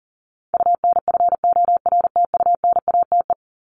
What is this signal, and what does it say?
Morse code, heard as a keyed tone, about 40 wpm, 700 Hz. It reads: UNFORTUNATE
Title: LY Suffix - Top 100 Words with Base Words 40wpm